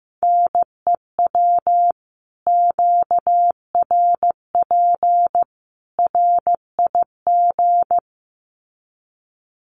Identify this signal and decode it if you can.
Morse code, heard as a keyed tone, about 15 wpm, 700 Hz. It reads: NEW QRP RIG
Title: Common QSO Phrases 15wpm